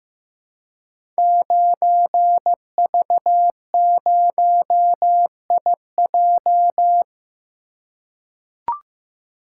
Morse code, heard as a keyed tone, about 15 wpm, 700 Hz. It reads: 9V0IJ
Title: Code Groups - 5 Letter:Numbers - 15wpm